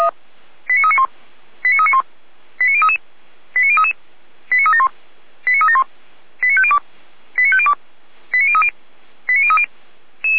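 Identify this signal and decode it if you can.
Five-tone selective call (ZVEI), heard as quick runs of five tones, tones of about 70 ms. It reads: 89391 89391 89033 89033 89371 89371 89502 89502 89030 89030
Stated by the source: ZVEI five-tone selective call received by SDR.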